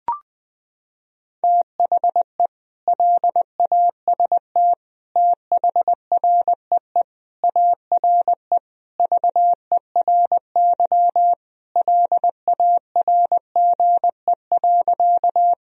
Morse code, heard as a keyed tone, about 20 wpm, 700 Hz. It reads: THE LAST THREE ARE VERY LARGE.
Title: Sentences from Top 1000 Words - Review Part 1 - 20wpm